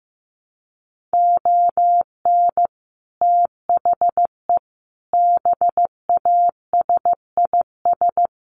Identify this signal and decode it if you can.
Morse code, heard as a keyed tone, about 15 wpm, 700 Hz. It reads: ON THE BASIS
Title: Sets of 5 Words Encore 15wpm